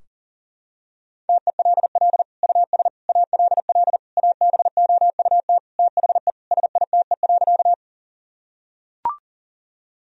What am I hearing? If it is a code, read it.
Morse code, heard as a keyed tone, about 37 wpm, 700 Hz. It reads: TELL US ALL ABOUT THE SITE.
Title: Sentences from Top 1000 Words - Review Part 4 - 40wpm